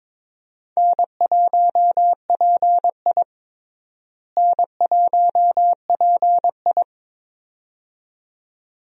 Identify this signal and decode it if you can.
Morse code, heard as a keyed tone, about 22 wpm, 700 Hz. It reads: N1PI N1PI
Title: USA Call Signs - 4 Characters - Repeated 2x - 22wpm